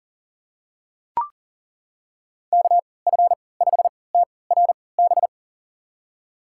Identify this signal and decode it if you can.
Morse code, heard as a keyed tone, about 37 wpm, spaced as for 22 wpm, 700 Hz. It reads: KF5TRB
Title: USA Call Signs - 6 Characters - 3x Character Spacing - 40wpm